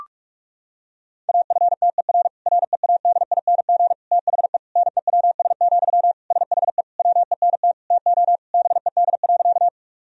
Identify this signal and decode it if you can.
Morse code, heard as a keyed tone, about 41 wpm, 700 Hz. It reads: AFTER READING THE NEWS, SHE WENT TO BED.
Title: Sentences from Top 1000 Words - Review Part 3 - 45wpm